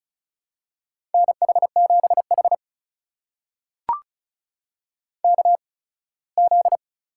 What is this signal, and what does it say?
Morse code, heard as a keyed tone, about 36 wpm, spaced as for 21 wpm, 700 Hz. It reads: NH7H KZ
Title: USA Call Signs - 4 Characters - 8x Character Spacing - 35wpm